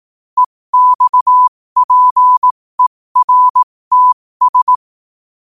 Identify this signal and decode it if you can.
Morse code, heard as a keyed tone, about 18 wpm, spaced as for 15 wpm, 1000 Hz. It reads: EXPERTS
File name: qotdfiles / 15wpm